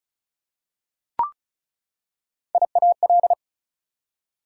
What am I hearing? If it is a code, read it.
Morse code, heard as a keyed tone, about 33 wpm, 700 Hz. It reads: IAL
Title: Prefixes and Suffixes 35wpm